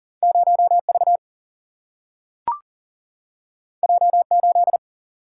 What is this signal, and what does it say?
Morse code, heard as a keyed tone, about 40 wpm, 700 Hz. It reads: 0 V J 8
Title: Single-Letter Single-Number 40wpm